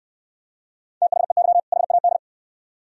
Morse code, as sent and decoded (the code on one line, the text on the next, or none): - .... -.-- / .... .- -..
THEY HAD